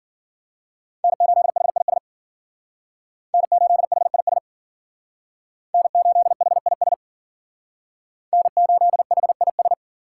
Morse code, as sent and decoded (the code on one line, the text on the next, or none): -. ---.. .... .. ... / -. ---.. .... .. ... / -. ---.. .... .. ... / -. ---.. .... .. ...
N8HIS N8HIS N8HIS N8HIS